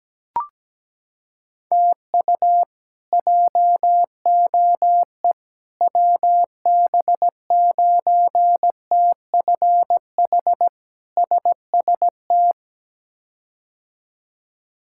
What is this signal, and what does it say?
Morse code, heard as a keyed tone, about 17 wpm, 700 Hz. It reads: TU JOE WB9TFH SST
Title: K1USN SST QSO Practice - 17wpm